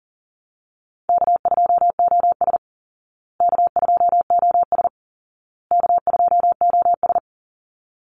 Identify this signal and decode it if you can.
Morse code, heard as a keyed tone, about 40 wpm, 700 Hz. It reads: K2OS K2OS K2OS
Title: Call Signs Repeated 2x - 40wpm